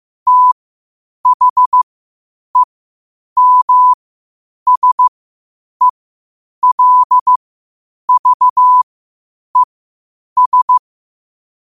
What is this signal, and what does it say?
Morse code, heard as a keyed tone, about 15 wpm, spaced as for 8 wpm, 1000 Hz. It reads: THEMSELVES
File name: qotdfiles / 8wpm